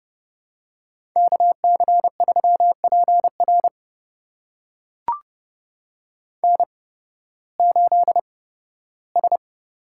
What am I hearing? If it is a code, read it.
Morse code, heard as a keyed tone, about 31 wpm, spaced as for 16 wpm, 700 Hz. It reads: KC3PR N8S
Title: USA Call Signs - 5 Characters - 8x Character Spacing - 30wpm